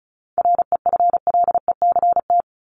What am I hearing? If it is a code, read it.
Morse code, heard as a keyed tone, about 34 wpm, 700 Hz. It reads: REFLECT
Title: Top 1251-1500 Words 35wpm